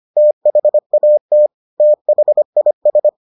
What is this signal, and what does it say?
Morse code, heard as a keyed tone, about 25 wpm, 600 Hz. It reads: THAT THIS